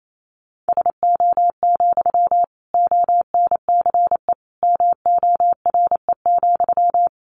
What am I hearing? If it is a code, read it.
Morse code, heard as a keyed tone, about 28 wpm, 700 Hz. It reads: SO, ONCE MORE,
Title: Sentences from Top 300 Words 28wpm